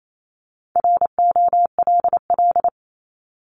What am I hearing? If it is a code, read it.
Morse code, heard as a keyed tone, about 28 wpm, 700 Hz. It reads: ROLL
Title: Multi-letter Phonemes 28wpm